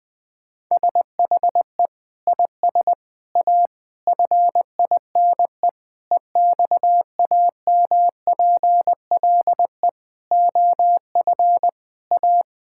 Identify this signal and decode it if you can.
Morse code, heard as a keyed tone, about 20 wpm, 700 Hz. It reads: SHE IS A FINE EXAMPLE OF A
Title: Sentences from Top 1000 Words - Review Part 6 - 20wpm